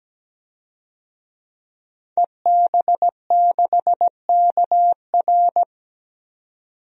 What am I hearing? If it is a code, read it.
Morse code, heard as a keyed tone, about 17 wpm, 700 Hz. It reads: EB6KR
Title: DXCC-100 Call Signs - 5 Characters - 17wpm